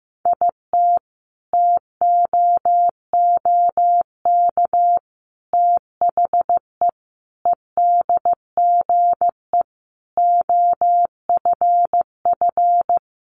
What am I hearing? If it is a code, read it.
Morse code, heard as a keyed tone, about 15 wpm, 700 Hz. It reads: IT TOOK THE EDGE OFF
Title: Sentences from Top 1300 Words 15wpm